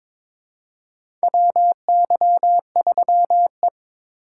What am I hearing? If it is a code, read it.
Morse code, heard as a keyed tone, about 22 wpm, 700 Hz. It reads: WY3E
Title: USA Call Signs - 4 Characters - 2x Character Spacing - 22wpm